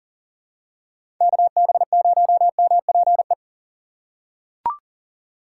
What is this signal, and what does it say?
Morse code, heard as a keyed tone, about 40 wpm, 700 Hz. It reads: KB0MPE
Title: USA Call Signs - 6 Characters - 2x Character Spacing - 40wpm